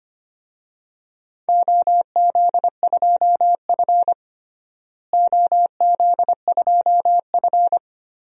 Morse code, heard as a keyed tone, about 25 wpm, 700 Hz. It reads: OZ2F OZ2F
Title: DXCC-100 Call Signs - 4 Characters - Repeated 3x - 25wpm